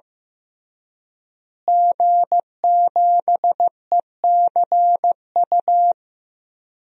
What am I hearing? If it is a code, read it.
Morse code, heard as a keyed tone, about 15 wpm, 700 Hz. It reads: G7ECU
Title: DXCC-100 Call Signs - 5 Characters - Repeated 3x - 15wpm